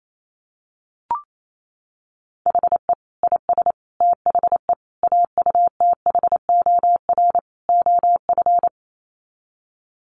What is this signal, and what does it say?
Morse code, heard as a keyed tone, about 28 wpm, 700 Hz. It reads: HE IS THE AUTHOR OF
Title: Sets of 5 Words Encore2 28wpm